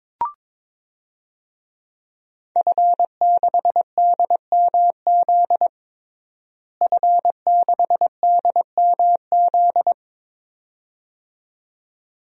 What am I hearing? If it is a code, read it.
Morse code, heard as a keyed tone, about 22 wpm, 700 Hz. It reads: F6DMZ F6DMZ
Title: DXCC-100 Call Signs - 5 Characters - Repeated 2x - 22wpm